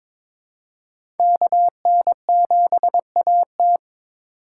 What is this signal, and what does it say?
Morse code, heard as a keyed tone, about 22 wpm, 700 Hz. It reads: KN7AT
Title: USA Call Signs - 5 Characters - 8x Character Spacing - 22wpm